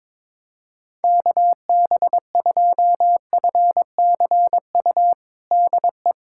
Morse code, heard as a keyed tone, about 21 wpm, 700 Hz. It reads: KB2FCU DE
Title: USA Callsign DE Callsign 22wpm